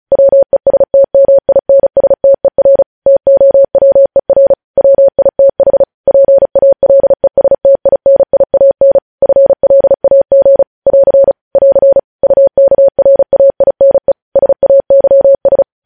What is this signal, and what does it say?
Morse code, heard as a keyed tone, about 36 wpm, 550 Hz. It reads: WESTMINSTER TOWER WITH PALESTINIAN FLAG <AR> <AR> UKRAINE SAYS
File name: podcast / News headlines-1x-35x35